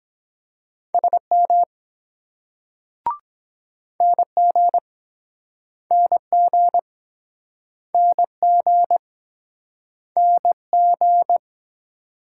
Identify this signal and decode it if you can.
Morse code, heard as a keyed tone, about 23 wpm, 700 Hz. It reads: SM NG NG NG NG